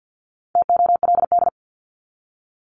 Morse code, heard as a keyed tone, about 48 wpm, 700 Hz. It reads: TOLD
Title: Top 500 Words 50wpm